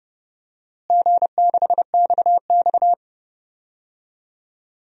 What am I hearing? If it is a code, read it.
Morse code, heard as a keyed tone, about 31 wpm, 700 Hz. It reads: G6XX
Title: DXCC-100 Call Signs - 4 Characters - Repeated 2x - 30wpm